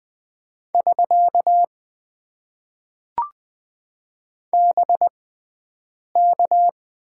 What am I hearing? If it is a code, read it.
Morse code, heard as a keyed tone, about 20 wpm, spaced as for 7 wpm, 700 Hz. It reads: <SK> BK
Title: Top 90 QSO Elements - 6x Character Spacing - 20wpm